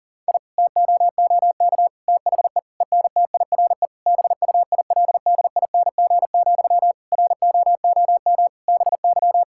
Morse code, heard as a keyed tone, about 40 wpm, 700 Hz. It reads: I TOOK THE ENTIRE BUILDING, ROOM BY
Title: Sentences from Top 1000 Words - Review Part 5 - 40wpm